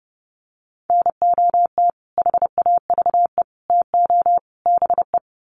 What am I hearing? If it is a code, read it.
Morse code, heard as a keyed tone, about 30 wpm, 700 Hz. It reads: NOT HAVE TO BE
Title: Sets of 4 Words 30wpm